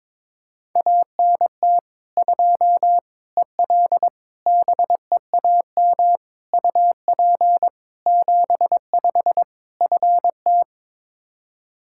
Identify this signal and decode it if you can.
Morse code, heard as a keyed tone, about 22 wpm, 700 Hz. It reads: ANT 2 EL BEAM UP 75 FT
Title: Practice QSO - Exchange 2 - Info - 22wpm